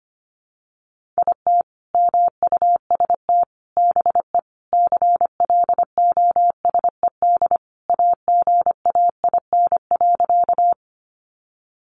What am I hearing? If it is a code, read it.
Morse code, heard as a keyed tone, about 25 wpm, 700 Hz. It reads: IT MUST BE CLOSED AGAIN.
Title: Sentences from Top 1400 Words 25wpm